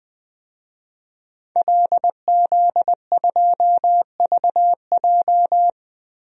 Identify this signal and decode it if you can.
Morse code, heard as a keyed tone, about 20 wpm, 700 Hz. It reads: LZ2VJ
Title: DXCC-100 Call Signs - 5 Characters - Repeated 2x - 20wpm